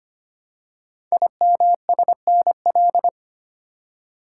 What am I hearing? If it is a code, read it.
Morse code, heard as a keyed tone, about 25 wpm, 700 Hz. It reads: IMSNL
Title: Code Groups - 5 Letters - 25wpm